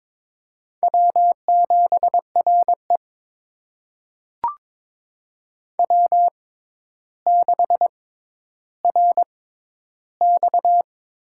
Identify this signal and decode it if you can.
Morse code, heard as a keyed tone, about 22 wpm, spaced as for 11 wpm, 700 Hz. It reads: W7RE W6RX
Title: USA Call Signs - 4 Characters - 6x Character Spacing - 22wpm